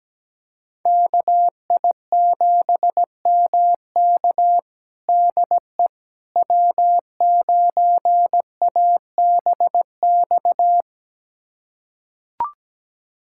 Morse code, heard as a keyed tone, about 17 wpm, 700 Hz. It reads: KI7MK DE W9ABX
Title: USA Callsign DE Callsign - 5 Characters - 17wpm